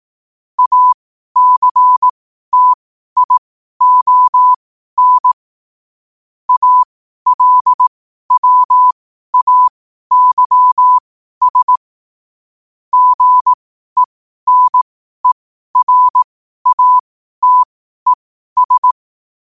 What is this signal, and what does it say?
Morse code, heard as a keyed tone, about 18 wpm, spaced as for 12 wpm, 1000 Hz. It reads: ACTION ALWAYS GENERATES
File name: qotdfiles / 12wpm